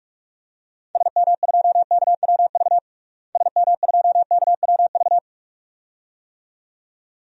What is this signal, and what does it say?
Morse code, heard as a keyed tone, about 43 wpm, 700 Hz. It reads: SM2KWV SM2KWV
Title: DXCC-100 Call Signs - 6 Characters - Repeated 2x - 45wpm